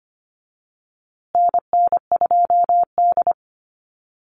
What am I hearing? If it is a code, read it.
Morse code, heard as a keyed tone, about 25 wpm, 700 Hz. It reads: NN2D
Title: USA Call Signs Repeated 2x 25wpm